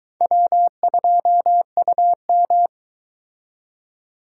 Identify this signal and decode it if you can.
Morse code, heard as a keyed tone, about 22 wpm, 700 Hz. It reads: W2UM